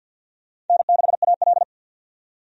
Morse code, heard as a keyed tone, about 52 wpm, 700 Hz. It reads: N6AL
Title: USA Call Signs - 4 Characters - 50wpm